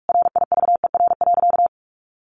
Morse code, heard as a keyed tone, about 44 wpm, 700 Hz. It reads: RIVER.
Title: Sentences from Top 1400 Words 45wpm